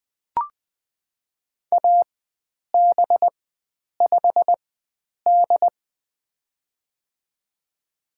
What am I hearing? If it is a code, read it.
Morse code, heard as a keyed tone, about 20 wpm, spaced as for 9 wpm, 700 Hz. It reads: AB5D
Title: USA Call Signs - 4 Characters - 4x Character Spacing - 20wpm